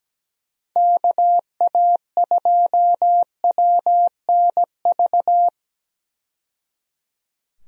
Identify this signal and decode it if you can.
Morse code, heard as a keyed tone, about 17 wpm, 700 Hz. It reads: KA2WNV
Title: USA Call Signs - 6 Characters - 17wpm